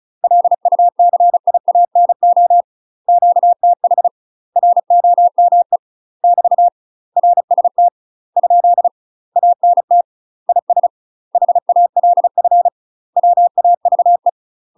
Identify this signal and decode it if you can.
Morse code, heard as a keyed tone, about 36 wpm, 700 Hz. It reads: LUCIANO QTH ROME <BT> RST ? ANT IS HALF WAVE
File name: UK MASTER QSOs TRAINING / 35WPM / UK MASTER QSOs 0018